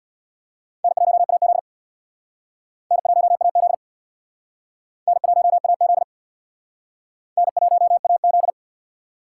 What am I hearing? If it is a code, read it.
Morse code, heard as a keyed tone, about 60 wpm, 690 Hz. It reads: N1AZ N1AZ N1AZ N1AZ
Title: Speed Racing - USA 1x2 Call Signs 50wpm